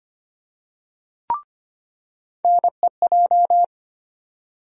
Morse code, heard as a keyed tone, about 25 wpm, 700 Hz. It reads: NEJ
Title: USA Call Sign Postfixes - 3 Letters 25wpm